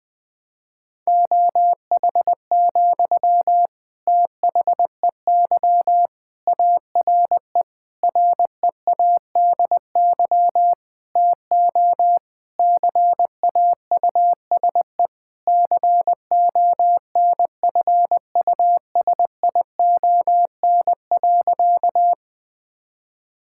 Morse code, heard as a keyed tone, about 20 wpm, 700 Hz. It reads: OH, THEY ARE READY TO CAUSE CONFUSION.
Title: Sentences from Top 1600 Words 20wpm